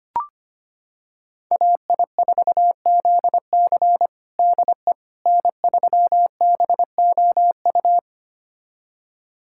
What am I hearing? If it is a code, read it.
Morse code, heard as a keyed tone, about 25 wpm, 700 Hz. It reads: AI4ZC DE N3BOU
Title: USA Callsign DE Callsign - 5 Characters - 25wpm